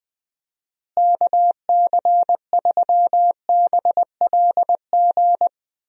Morse code, heard as a keyed tone, about 20 wpm, 700 Hz. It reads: KC3BLG
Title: USA Call Signs - 6 Characters - Repeated 3x - 20wpm